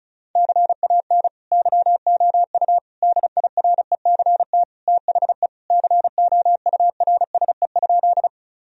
Morse code, heard as a keyed tone, about 36 wpm, 700 Hz. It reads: CAN YOU DIRECT THE COURSE?
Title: Sentences from Top 1000 Words - Review Part 2 - 35wpm